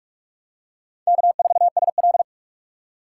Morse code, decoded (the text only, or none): K4SL